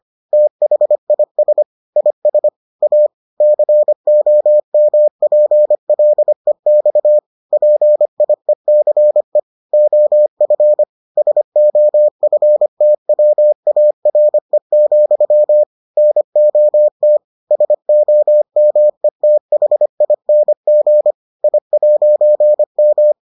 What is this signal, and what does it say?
Morse code, heard as a keyed tone, about 25 wpm, 600 Hz. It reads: THIS IS A COMPLEX PIECE OF SOFTWARE, NOT SOMETHING I'M